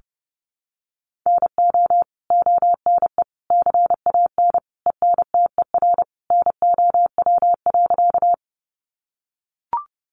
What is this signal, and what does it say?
Morse code, heard as a keyed tone, about 30 wpm, 700 Hz. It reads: NO ONE CAN ENTER NOW.
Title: Sentences from Top 1300 Words 30wpm